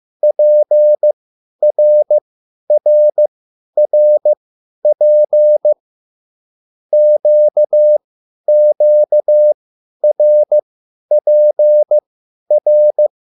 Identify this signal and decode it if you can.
Morse code, heard as a keyed tone, about 15 wpm, spaced as for 10 wpm, 600 Hz. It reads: PRRRP QQRPR